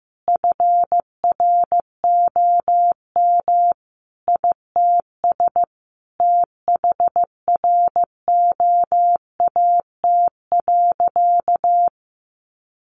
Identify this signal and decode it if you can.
Morse code, heard as a keyed tone, about 15 wpm, 700 Hz. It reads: FROM ITS THROAT.
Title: A Sound Of Thunder 15wpm